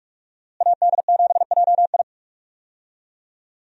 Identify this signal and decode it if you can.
Morse code, heard as a keyed tone, about 45 wpm, 700 Hz. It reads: AD7JI
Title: USA Call Signs - 5 Characters - Repeated 2x - 45wpm